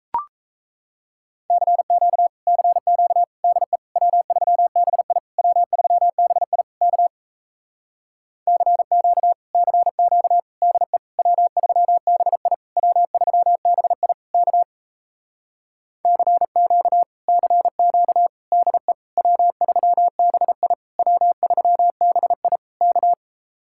Morse code, heard as a keyed tone, about 37 wpm, 700 Hz. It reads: CQ CQ DE W3BI W3BI K CQ CQ DE W3BI W3BI K CQ CQ DE W3BI W3BI K